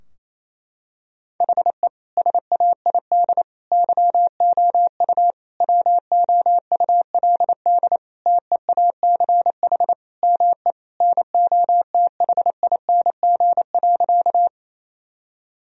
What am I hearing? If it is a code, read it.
Morse code, heard as a keyed tone, about 28 wpm, 700 Hz. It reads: HE SAID YOU WOULD TEACH ME NOTHING.
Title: Sentences from Top 1000 Words - Review Part 2 - 28wpm